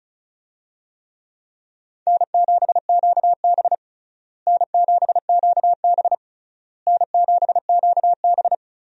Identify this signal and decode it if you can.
Morse code, heard as a keyed tone, about 36 wpm, 700 Hz. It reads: N7QB N7QB N7QB
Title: USA Call Signs - 4 Characters - Repeated 3x - 35wpm